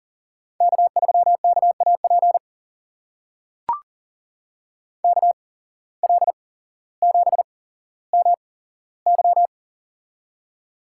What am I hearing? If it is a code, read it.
Morse code, heard as a keyed tone, about 38 wpm, spaced as for 17 wpm, 700 Hz. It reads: K3KAP KL7MY
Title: USA Call Signs - 5 Characters - 8x Character Spacing - 40wpm